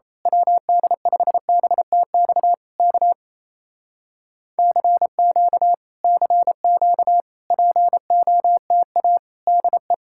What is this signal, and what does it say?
Morse code, heard as a keyed tone, about 27 wpm, 700 Hz. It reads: WD5BTX K CQ CQ POTA DE
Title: Speed-Racing Common QSO Phrases Encore - 28wpm